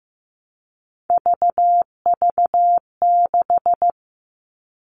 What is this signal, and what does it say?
Morse code, heard as a keyed tone, about 15 wpm, 700 Hz. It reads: VV6
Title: All Cut Numbers - 3-Digit 15wpm